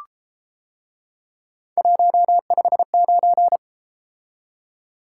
Morse code, heard as a keyed tone, about 33 wpm, 700 Hz. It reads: 159